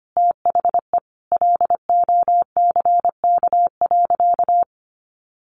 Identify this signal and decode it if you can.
Morse code, heard as a keyed tone, about 25 wpm, 700 Hz. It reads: THE LOCK.
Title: Sentences from Top 1300 Words 25wpm